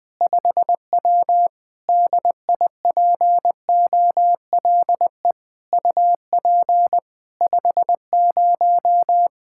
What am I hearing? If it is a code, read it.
Morse code, heard as a keyed tone, about 20 wpm, 700 Hz. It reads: 5W DIPOLE UP 50
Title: Practice QSO - Exchange 2 - Info - 20wpm